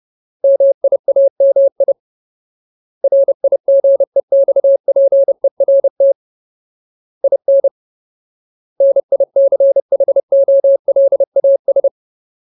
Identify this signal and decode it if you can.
Morse code, heard as a keyed tone, about 30 wpm, 550 Hz. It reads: MIAMI RIGEXPERT IN NICHOLAS
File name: Ultimate Word List / 30WPM Ultimate